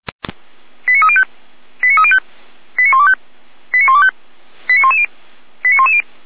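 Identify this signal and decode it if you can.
Five-tone selective call (ZVEI), heard as quick runs of five tones, tones of about 70 ms. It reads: 89305 89305 89125 89125 89110 89110
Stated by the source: ZVEI five-tone call 89042 received via SDR, on the Feuerwehr Orlamuende audio server.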